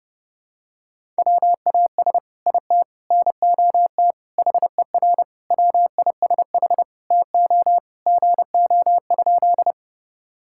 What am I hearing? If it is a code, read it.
Morse code, heard as a keyed tone, about 32 wpm, 700 Hz. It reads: WAS IT NOT HER WISH TO GO?
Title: Sentences from Top 1600 Words 30wpm